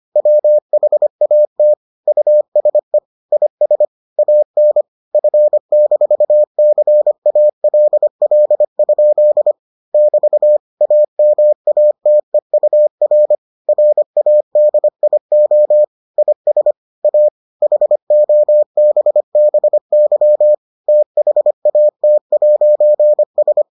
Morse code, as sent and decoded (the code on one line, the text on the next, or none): .-- .... .- - / ..- ... / .. ... / .- -. / ..-. -....- -.-. .- .-.. .-.. ..--.. / -...- / .- -- .- - . ..- .-. / .-. .- -.. .. --- / .. ... / .- / .... --- -... -... -.-- / - .... .- - .----. ...
WHAT USE IS AN F-CALL? <BT> AMATEUR RADIO IS A HOBBY THAT'S